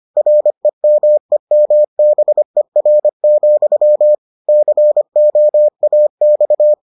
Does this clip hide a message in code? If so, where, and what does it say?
Yes: Morse code from 0.2 to 6.7 s, heard as a keyed tone, about 25 wpm, 600 Hz. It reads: REMEMBER, COAX